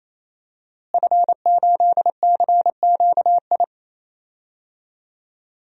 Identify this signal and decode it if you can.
Morse code, heard as a keyed tone, about 28 wpm, 700 Hz. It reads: F8CQI
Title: DXCC-100 Call Signs - 5 Characters - Repeated 3x - 28wpm